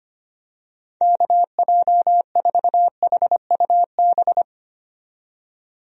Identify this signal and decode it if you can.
Morse code, heard as a keyed tone, about 25 wpm, 700 Hz. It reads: KJ4HUB
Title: USA Call Signs - 6 Characters - 25wpm